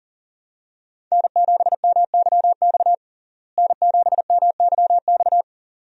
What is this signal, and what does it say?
Morse code, heard as a keyed tone, about 40 wpm, 700 Hz. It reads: N7MYX N7MYX
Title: USA Call Signs - 5 Characters - Repeated 2x - 40wpm